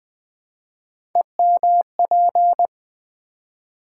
Morse code, heard as a keyed tone, about 20 wpm, 700 Hz. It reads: EMP
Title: Build A Word - 3 Letters At A Time - 9-Letter Words - 20wpm